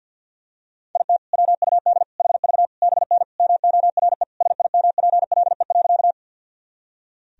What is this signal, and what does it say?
Morse code, heard as a keyed tone, about 47 wpm, 700 Hz. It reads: IT WUD HV BN MORE SIMPLE.
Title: Sentences from Top 400 Words and Abbreviations 50wpm